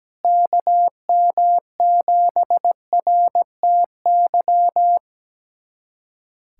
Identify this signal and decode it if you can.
Morse code, heard as a keyed tone, about 17 wpm, 700 Hz. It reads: KM7RTY
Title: USA Callsign DE Callsign 17wpm